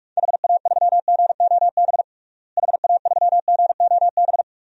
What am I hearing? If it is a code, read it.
Morse code, heard as a keyed tone, about 41 wpm, 700 Hz. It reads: HA3GOB HA3GOB
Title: DXCC-100 Call Signs - 6 Characters - Repeated 3x - 45wpm